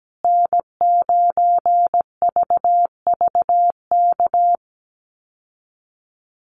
Morse code, heard as a keyed tone, about 17 wpm, 700 Hz. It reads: N9VVK
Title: USA Call Signs Repeated 2x 17wpm